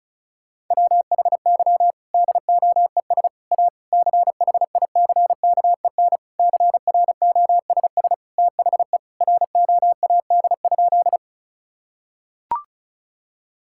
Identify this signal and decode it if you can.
Morse code, heard as a keyed tone, about 34 wpm, 700 Hz. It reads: WHY DOES A CHICKEN CROSS THE ROAD?
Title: Sentences from Top 1000 Words - Review Part 5 - 35wpm